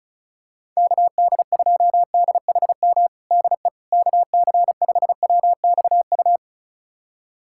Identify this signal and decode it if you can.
Morse code, heard as a keyed tone, about 34 wpm, 700 Hz. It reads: KD2DHM DE KC5WXU
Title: USA Callsign DE Callsign - 6 Characters - 35wpm